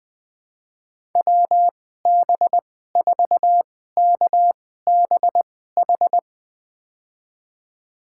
Morse code, heard as a keyed tone, about 20 wpm, spaced as for 14 wpm, 700 Hz. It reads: WB4KBH
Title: USA Call Signs - 6 Characters - 2x Character Spacing - 20wpm